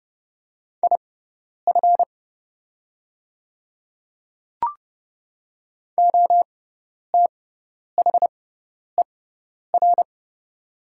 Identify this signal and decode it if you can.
Morse code, heard as a keyed tone, about 30 wpm, spaced as for 10 wpm, 700 Hz. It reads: IF OTHER
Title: 6x Character Spacing - Top 100 Words - 30wpm